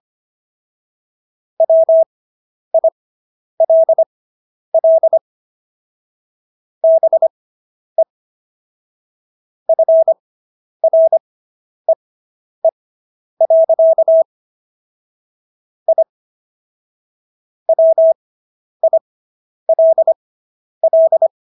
Morse code, heard as a keyed tone, about 25 wpm, spaced as for 10 wpm, 650 Hz. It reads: WILL BE FREE. I WILL